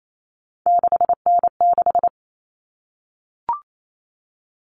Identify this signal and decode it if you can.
Morse code, heard as a keyed tone, about 28 wpm, 700 Hz. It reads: 6N6
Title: All Cut Numbers - 3-Digit 28wpm